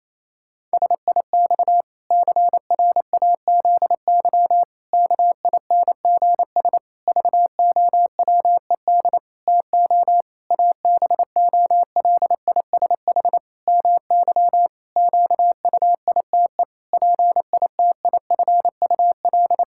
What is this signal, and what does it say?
Morse code, heard as a keyed tone, about 28 wpm, 700 Hz. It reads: SIX CRAZY KINGS VOWED TO ABOLISH MY QUITE PITIFUL
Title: Pangram Challenge v2 - 28wpm